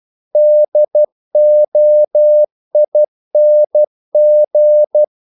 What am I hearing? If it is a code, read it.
Morse code, heard as a keyed tone, about 12 wpm, 600 Hz. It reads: DOING
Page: English quotations in CW (telegraphy - morse)